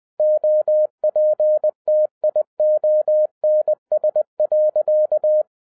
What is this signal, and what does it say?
Morse code, heard as a keyed tone, about 20 wpm, 610 Hz. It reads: OPTIONS.
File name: 2011-07-mp3 / n43 Czech Army Morse Keys 20wpm